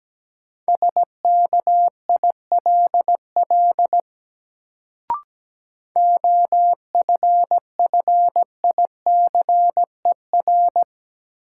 Morse code, heard as a keyed tone, about 17 wpm, 700 Hz. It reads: SKILL OFFICER
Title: Sentences from Top 1200 Words - New Word Drill Alt - 17wpm